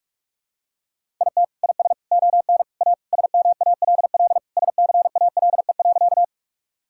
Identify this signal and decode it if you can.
Morse code, heard as a keyed tone, about 45 wpm, 700 Hz. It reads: IT IS ON A SMALL SCALE.
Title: Sentences from Top 1000 Words - Review Part 5 - 45wpm